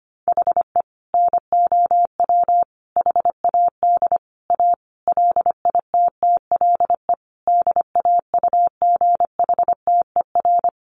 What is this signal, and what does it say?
Morse code, heard as a keyed tone, about 25 wpm, 700 Hz. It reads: HE NOW HAD A LITTLE DAUGHTER
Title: Sentences from Top 1500 Words 25wpm